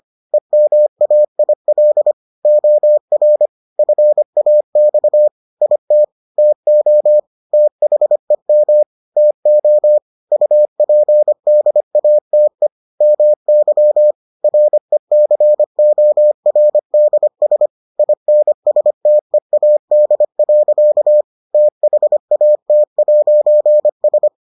Morse code, heard as a keyed tone, about 25 wpm, 600 Hz. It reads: EMAIL OR FAX IT TO THEM TO UPDATE MY RECORDS INSTEAD. THAT'S